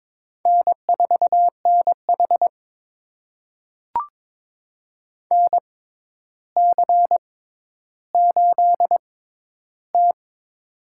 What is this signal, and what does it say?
Morse code, heard as a keyed tone, about 22 wpm, spaced as for 11 wpm, 700 Hz. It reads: N4NH NC8T
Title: USA Call Signs - 4 Characters - 6x Character Spacing - 22wpm